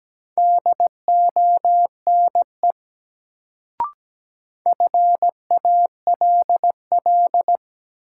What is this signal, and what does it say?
Morse code, heard as a keyed tone, about 17 wpm, 700 Hz. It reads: DONE FALL
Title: Sentences from Top 500 Words - New Word Drill - 17wpm